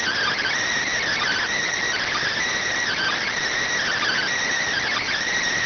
Emergency Alert System (SAME) tones, heard as harsh buzzing bursts. It is heard from 1.2 to 5.1 s.